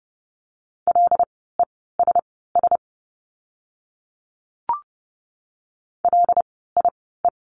Morse code, heard as a keyed tone, about 30 wpm, spaced as for 17 wpm, 700 Hz. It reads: LESS LIE
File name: Lesson 16f - Letter c - Words 30wpm